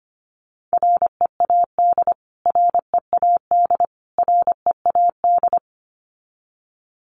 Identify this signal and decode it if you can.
Morse code, heard as a keyed tone, about 25 wpm, 700 Hz. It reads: READ READ READ
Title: Top 500 Words Repeated 25wpm